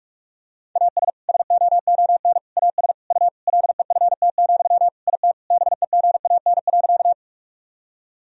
Morse code, heard as a keyed tone, about 46 wpm, 700 Hz. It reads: AS SOON AS U LEFT, IT BEGAN.
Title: Sentences from Top 400 Words and Abbreviations 45wpm